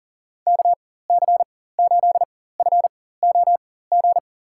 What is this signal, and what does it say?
Morse code, heard as a keyed tone, about 40 wpm, spaced as for 19 wpm, 700 Hz. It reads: KC8FOG
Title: USA Call Signs - 6 Characters - 4x Character Spacing - 40wpm